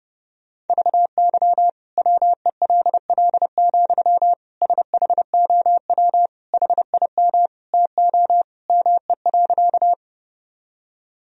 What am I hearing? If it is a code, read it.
Morse code, heard as a keyed tone, about 30 wpm, 700 Hz. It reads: VY WELL, SHOW HIM TO ME.
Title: Sentences from Top 200 Words and Abbreviations 30wpm